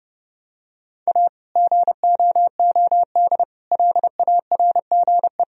Morse code, heard as a keyed tone, about 30 wpm, 700 Hz. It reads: A GOOD LARGE
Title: Sentences from Top 1000 Words - Review Part 2 - 30wpm